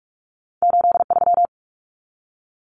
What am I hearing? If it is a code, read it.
Morse code, heard as a keyed tone, about 46 wpm, 700 Hz. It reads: 83
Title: Numbers - 2-Digit 45wpm